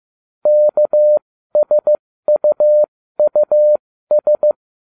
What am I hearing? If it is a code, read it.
Morse code, heard as a keyed tone, about 15 wpm, spaced as for 13 wpm, 600 Hz. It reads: KSUUS